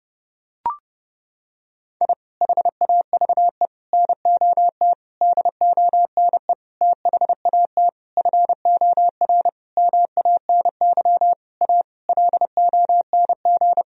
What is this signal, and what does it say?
Morse code, heard as a keyed tone, about 30 wpm, 700 Hz. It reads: I HAVE NOT DONE THAT FOR MANY A LONG
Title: Sentences from Top 1000 Words - Review Part 1 - 30wpm